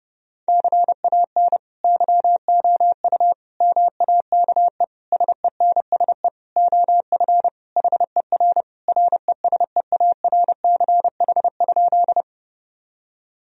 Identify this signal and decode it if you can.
Morse code, heard as a keyed tone, about 30 wpm, 700 Hz. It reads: CAN YOU MAKE SENSE OF HER RESEARCH?
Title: Sentences from Top 1000 Words - Review Part 4 - 30wpm